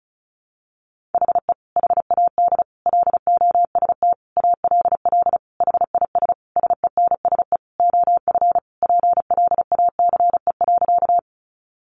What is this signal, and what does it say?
Morse code, heard as a keyed tone, about 34 wpm, 700 Hz. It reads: HE HAD LOST ALL HIS SENSE OF PLACE.
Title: Sentences from Top 1400 Words 35wpm